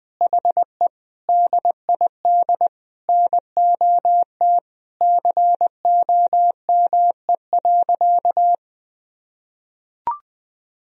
Morse code, heard as a keyed tone, about 20 wpm, 700 Hz. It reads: HE DID NOT COME.
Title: Sentences from Top 1000 Words - Review Part 4 - 20wpm